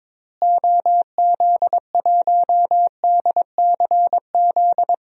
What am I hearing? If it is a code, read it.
Morse code, heard as a keyed tone, about 23 wpm, 700 Hz. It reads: OZ1DCZ
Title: DXCC-100 Call Signs - 6 Characters - Repeated 2x - 22wpm